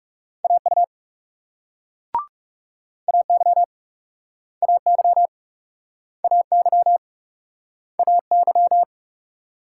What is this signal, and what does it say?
Morse code, heard as a keyed tone, about 42 wpm, 700 Hz. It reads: AU AY AY AY AY